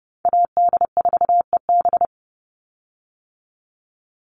Morse code, heard as a keyed tone, about 30 wpm, 700 Hz. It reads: AD4EB
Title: Call Signs - 30wpm